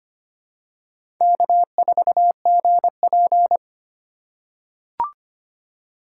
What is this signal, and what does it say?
Morse code, heard as a keyed tone, about 25 wpm, 700 Hz. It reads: K4GP
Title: USA Call Signs - 4 Characters - 4x Character Spacing - 25wpm